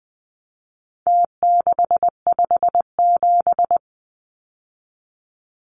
Morse code, heard as a keyed tone, about 20 wpm, 700 Hz. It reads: T657
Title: Numbers - Contest Style - 4-Digit 20wpm